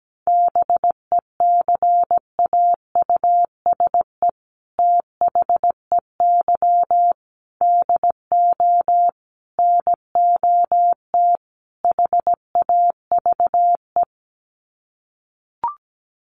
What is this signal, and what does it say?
Morse code, heard as a keyed tone, about 17 wpm, 700 Hz. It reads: BECAUSE THEY DO NOT HAVE
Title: Sets of 5 Words Encore2 17wpm